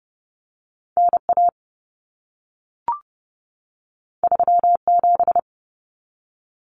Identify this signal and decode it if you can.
Morse code, heard as a keyed tone, about 30 wpm, 700 Hz. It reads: NA 37
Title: Common Cut Numbers - 2-Digit 30wpm